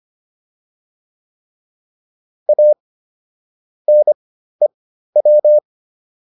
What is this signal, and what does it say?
Morse code, heard as a keyed tone, about 25 wpm, spaced as for 13 wpm, 600 Hz. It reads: A NEW